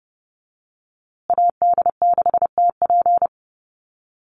Morse code, heard as a keyed tone, about 29 wpm, 700 Hz. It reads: AD6TP
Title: USA Call Signs Repeated 3x 30wpm